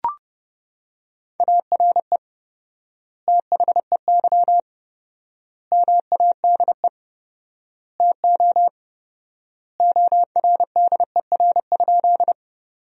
Morse code, heard as a keyed tone, about 30 wpm, 700 Hz. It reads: ARE THEY MADE TO ORDER?
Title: 4x Word-Spacing - Sentences from Top 400 Words - 30wpm